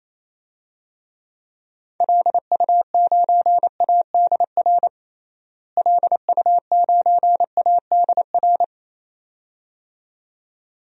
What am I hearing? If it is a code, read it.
Morse code, heard as a keyed tone, about 28 wpm, 700 Hz. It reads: LU9ADR LU9ADR
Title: DXCC-100 Call Signs - 6 Characters - Repeated 2x - 28wpm